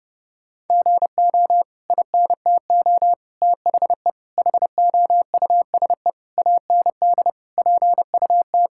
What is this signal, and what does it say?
Morse code, heard as a keyed tone, about 30 wpm, 700 Hz. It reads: GO INTO THE HOUSE AND PUT
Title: Sentences from Top 1000 Words - Review Part 5 - 30wpm